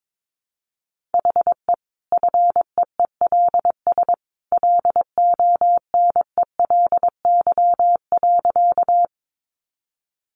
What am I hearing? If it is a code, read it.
Morse code, heard as a keyed tone, about 22 wpm, 700 Hz. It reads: HE FEELS LONELY.
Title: Easy Phrases 22wpm